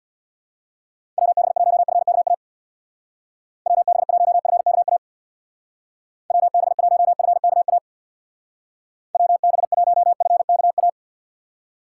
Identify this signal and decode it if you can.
Morse code, heard as a keyed tone, about 62 wpm, 700 Hz. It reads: WB1FKA WB1FKA WB1FKA WB1FKA
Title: Speed Racing - USA Call Signs 50wpm